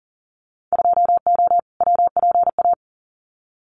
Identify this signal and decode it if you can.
Morse code, heard as a keyed tone, about 37 wpm, 700 Hz. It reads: 2O WPA
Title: Winter Field Day Exchanges 40wpm